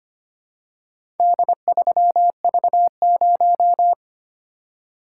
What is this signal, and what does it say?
Morse code, heard as a keyed tone, about 25 wpm, 700 Hz. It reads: D3V0
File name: practice / 25wpm / Four Letter-Number Combinations - 25wpm